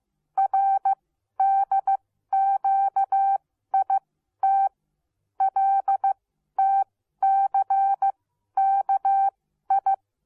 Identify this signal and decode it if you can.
Morse code, heard as a keyed tone, about 15 wpm, spaced as for 12 wpm, 780 Hz. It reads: RDQIT LTCKI